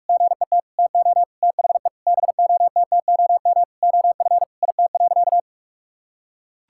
Morse code, heard as a keyed tone, about 44 wpm, 700 Hz. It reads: GET TO THE BOTTOM OF IT.
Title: Sentences from Top 1000 Words - Review Part 6 - 45wpm